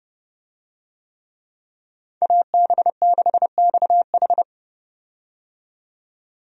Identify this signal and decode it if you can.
Morse code, heard as a keyed tone, about 32 wpm, 700 Hz. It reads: AB6XH
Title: USA Call Signs - 5 Characters - 30wpm